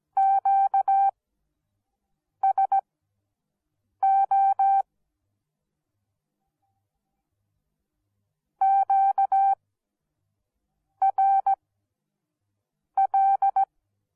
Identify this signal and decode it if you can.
Morse code, heard as a keyed tone, about 17 wpm, spaced as for 6 wpm, 780 Hz. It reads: QSO QRL